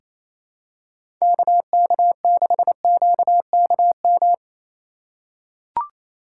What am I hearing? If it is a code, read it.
Morse code, heard as a keyed tone, about 28 wpm, 700 Hz. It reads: KK6QKM
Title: USA Call Signs - 6 Characters - 2x Character Spacing - 28wpm